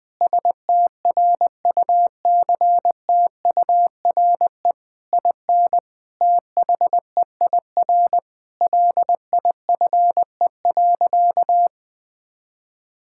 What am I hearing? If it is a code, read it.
Morse code, heard as a keyed tone, about 21 wpm, 700 Hz. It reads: STRUCTURE IN THEIR LIFE.
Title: Sentences from Top 1000 Words - Review Part 6 - 20wpm